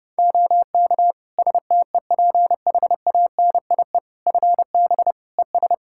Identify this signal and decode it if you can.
Morse code, heard as a keyed tone, about 30 wpm, 700 Hz. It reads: OK STEPHANIE FB ES
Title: Practice QSO - Exchange 2 - Info - 30wpm